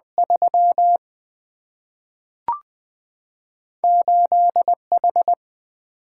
Morse code, heard as a keyed tone, about 20 wpm, 700 Hz. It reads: 3 8H
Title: Single-Letter Single-Number 20wpm